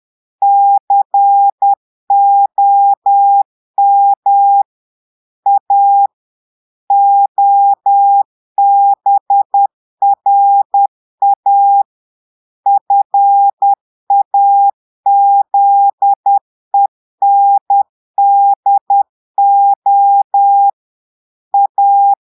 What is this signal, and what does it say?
Morse code, heard as a keyed tone, about 10 wpm, 800 Hz. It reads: COM A OBRA FAZENDO A
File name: Portugais - texte en clair 1 - vitesse - 10 mots minute